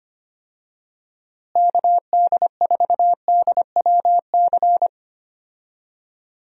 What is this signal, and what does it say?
Morse code, heard as a keyed tone, about 25 wpm, 700 Hz. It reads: KD4DWC
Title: USA Call Signs - 6 Characters - Repeated 3x - 25wpm